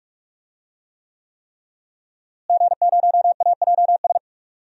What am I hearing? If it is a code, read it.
Morse code, heard as a keyed tone, about 42 wpm, 700 Hz. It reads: G0AJS
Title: DXCC-100 Call Signs - 5 Characters - Repeated 3x - 45wpm